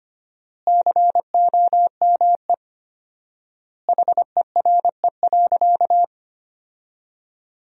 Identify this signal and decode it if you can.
Morse code, heard as a keyed tone, about 25 wpm, 700 Hz. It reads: COME HERE.
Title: 4x Word-Spacing - Sentences from Top 300 Words - 25wpm